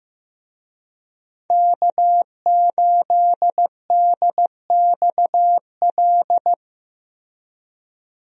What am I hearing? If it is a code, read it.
Morse code, heard as a keyed tone, about 15 wpm, 700 Hz. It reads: K8DXL
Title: USA Call Signs - 5 Characters - Repeated 3x - 15wpm